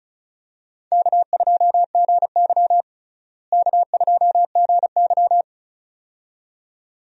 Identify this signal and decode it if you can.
Morse code, heard as a keyed tone, about 36 wpm, 700 Hz. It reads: K2GY K2GY
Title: USA Call Signs - 4 Characters - Repeated 2x - 35wpm